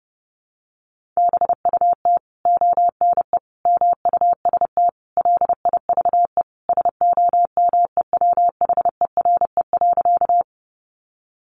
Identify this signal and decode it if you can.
Morse code, heard as a keyed tone, about 30 wpm, 700 Hz. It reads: BUT ONE MUST LIVE SOMEWHERE.
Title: Sentences from Top 1400 Words 30wpm